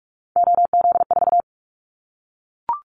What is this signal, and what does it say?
Morse code, heard as a keyed tone, about 44 wpm, 700 Hz. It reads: OZ4
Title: Callsign Prefixes for Top 100 DXCC - 45wpm